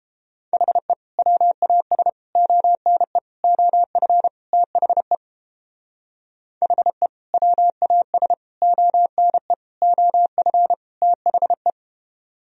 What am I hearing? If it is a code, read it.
Morse code, heard as a keyed tone, about 29 wpm, 700 Hz. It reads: HE WAS ONE OF THE HE WAS ONE OF THE